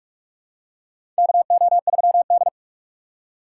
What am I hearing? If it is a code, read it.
Morse code, heard as a keyed tone, about 46 wpm, 700 Hz. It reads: KO3D
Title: USA Call Signs - 4 Characters - 2x Character Spacing - 45wpm